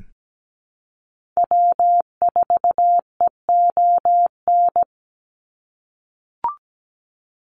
Morse code, heard as a keyed tone, about 17 wpm, 700 Hz. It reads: W4EON
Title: NA SOTA Activator Callsigns 17wpm